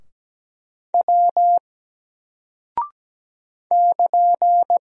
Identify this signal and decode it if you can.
Morse code, heard as a keyed tone, about 17 wpm, 700 Hz. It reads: W <KN>
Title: Single Character (Letter:Number:Punctuation:Prosign) 17wpm